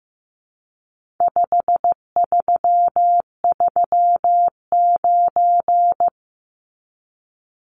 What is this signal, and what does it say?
Morse code, heard as a keyed tone, about 15 wpm, 700 Hz. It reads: 5339
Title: Numbers - 4-Digit 15wpm